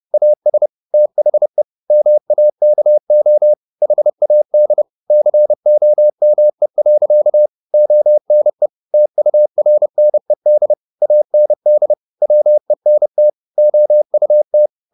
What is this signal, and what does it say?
Morse code, heard as a keyed tone, about 30 wpm, 600 Hz. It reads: AS THE MAKO HAD COME. ONE TURNED AND WENT OUT